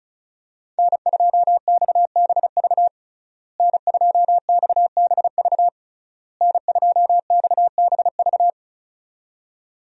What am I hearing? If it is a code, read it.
Morse code, heard as a keyed tone, about 34 wpm, 700 Hz. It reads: N2XBV N2XBV N2XBV
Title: USA Call Signs - 5 Characters - Repeated 3x - 35wpm